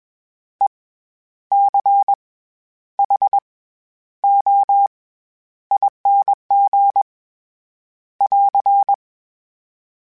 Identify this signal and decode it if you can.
Morse code, heard as a keyed tone, about 21 wpm, 800 Hz. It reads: E C H O ING <AR>
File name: Special INGs-mp3s / Special INGs-mp3s / SlowToFaster ing @20